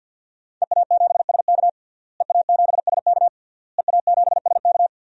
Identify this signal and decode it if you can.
Morse code, heard as a keyed tone, about 50 wpm, 700 Hz. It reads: EA7SK EA7SK EA7SK
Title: DXCC-100 Call Signs - 5 Characters - Repeated 3x - 50wpm